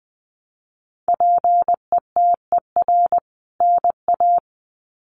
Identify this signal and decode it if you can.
Morse code, heard as a keyed tone, about 20 wpm, 700 Hz. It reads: PETER NA
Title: CWT Exchange 20wpm